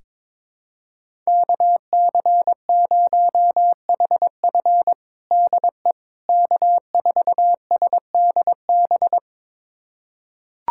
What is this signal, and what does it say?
Morse code, heard as a keyed tone, about 22 wpm, 700 Hz. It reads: KC0HF DE K4SDB
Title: USA Callsign DE Callsign - 5 Characters - 22wpm